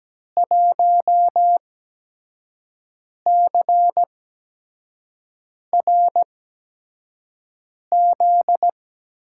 Morse code, heard as a keyed tone, about 17 wpm, spaced as for 5 wpm, 700 Hz. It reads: 1CRZ
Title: USA Call Signs - 5 Characters - 8x Character Spacing - 17wpm